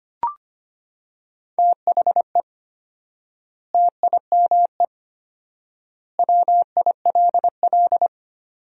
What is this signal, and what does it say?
Morse code, heard as a keyed tone, about 25 wpm, 700 Hz. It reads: THE TIME WILL
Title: 4x Word-Spacing - Sentences from Top 100 Words - 25wpm